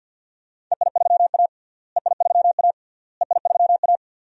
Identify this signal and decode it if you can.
Morse code, heard as a keyed tone, about 50 wpm, 700 Hz. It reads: EI3A EI3A EI3A
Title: DXCC-100 Call Signs - 4 Characters - Repeated 3x - 50wpm